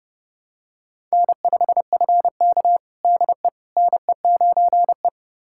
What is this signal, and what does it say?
Morse code, heard as a keyed tone, about 30 wpm, 700 Hz. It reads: N5FK DE NE9E
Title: USA Callsign DE Callsign - 4 Characters - 30wpm